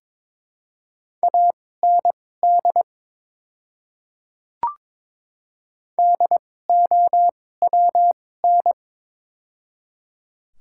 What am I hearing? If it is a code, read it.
Morse code, heard as a keyed tone, about 21 wpm, spaced as for 16 wpm, 700 Hz. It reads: AND DOWN
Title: 2x Character Spacing - Top 100 Words - 22wpm